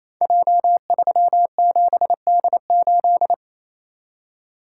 Morse code, heard as a keyed tone, about 27 wpm, 700 Hz. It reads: J37D8
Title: Code Groups - 5 Letter:Numbers - 28wpm